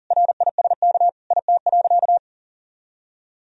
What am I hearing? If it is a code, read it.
Morse code, heard as a keyed tone, about 37 wpm, 700 Hz. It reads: RISK IT.
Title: Sentences from Top 1000 Words - Review Part 4 - 40wpm